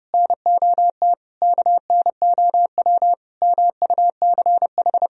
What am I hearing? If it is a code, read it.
Morse code, heard as a keyed tone, about 30 wpm, 700 Hz. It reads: NOT KNOW MUCH
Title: Sentences from Top 1000 Words - Review Part 2 - 30wpm